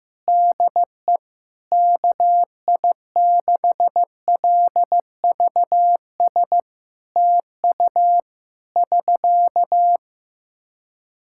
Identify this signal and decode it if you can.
Morse code, heard as a keyed tone, about 15 wpm, 700 Hz. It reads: DE KI6LVS TU <SK>
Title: Practice QSO - Exchange 4 - Ending - 15wpm